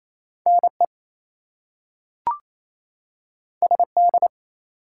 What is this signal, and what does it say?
Morse code, heard as a keyed tone, about 28 wpm, 700 Hz. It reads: NE SD
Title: US States and Canadian Provinces 28wpm